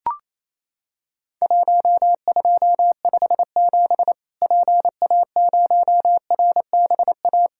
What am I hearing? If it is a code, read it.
Morse code, heard as a keyed tone, about 28 wpm, 700 Hz. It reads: 1257 PA0RBA
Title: Worked All Europe DX Contest - QTC Exchange - 28wpm